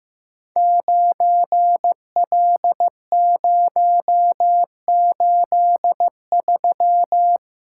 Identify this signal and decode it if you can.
Morse code, heard as a keyed tone, about 15 wpm, 700 Hz. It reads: 9L083
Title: Code Groups - 5 Letter:Numbers - 15wpm